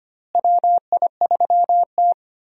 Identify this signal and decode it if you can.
Morse code, heard as a keyed tone, about 25 wpm, 700 Hz. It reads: WI3T
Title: USA Call Signs - 4 Characters - Repeated 3x - 25wpm